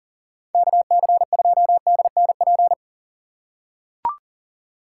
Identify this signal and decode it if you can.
Morse code, heard as a keyed tone, about 40 wpm, 700 Hz. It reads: KC2DNP
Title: USA Call Signs - 6 Characters - 2x Character Spacing - 40wpm